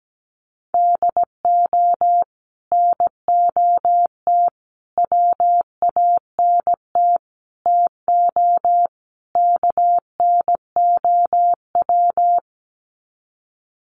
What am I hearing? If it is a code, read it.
Morse code, heard as a keyed tone, about 17 wpm, 700 Hz. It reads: DO NOT WANT TO KNOW
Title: Sets of 5 Words Encore 17wpm